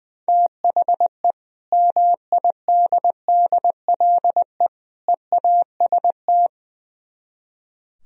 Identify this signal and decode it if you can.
Morse code, heard as a keyed tone, about 22 wpm, 700 Hz. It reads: THE MIDDLE EAST
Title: Speed-Racing - Sets of 3 Words 20wpm